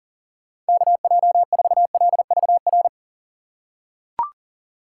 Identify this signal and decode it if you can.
Morse code, heard as a keyed tone, about 37 wpm, 690 Hz. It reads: KJ4LVR
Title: USA Call Signs - 6x Character Spacing - 40wpm